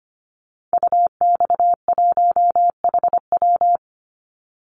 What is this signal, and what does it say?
Morse code, heard as a keyed tone, about 25 wpm, 700 Hz. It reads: UX1HW
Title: Call Signs Repeated 2x - 25wpm